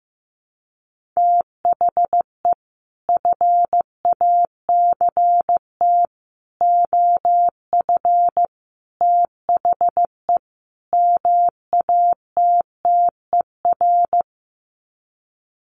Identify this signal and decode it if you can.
Morse code, heard as a keyed tone, about 15 wpm, 700 Hz. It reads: THE FACT OF THE MATTER
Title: Sets of 5 Words 15wpm